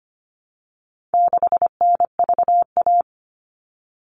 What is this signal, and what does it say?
Morse code, heard as a keyed tone, about 25 wpm, 700 Hz. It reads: 6NVA
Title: All Cut Numbers - 4-Digit 25wpm